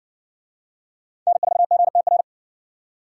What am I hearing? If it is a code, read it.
Morse code, heard as a keyed tone, about 65 wpm, 700 Hz. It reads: N4GTR